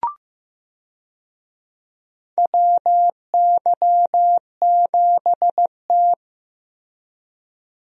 Morse code, heard as a keyed tone, about 15 wpm, 700 Hz. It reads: WY7T
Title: USA Call Signs - 4 Characters - Repeated 2x - 15wpm